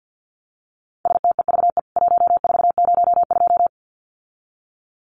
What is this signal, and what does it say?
Morse code, heard as a keyed tone, about 50 wpm, 700 Hz. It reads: STEVE 1402
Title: CWT Exchange 50wpm